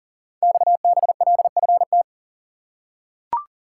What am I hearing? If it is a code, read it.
Morse code, heard as a keyed tone, about 40 wpm, 700 Hz. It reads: XBLFT
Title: Code Groups - 5 Letters - 40wpm